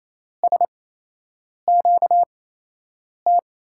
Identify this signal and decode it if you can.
Morse code, heard as a keyed tone, about 28 wpm, spaced as for 8 wpm, 700 Hz. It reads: SQT
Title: USA Call Signs - 6 Characters - 8x Character Spacing - 28wpm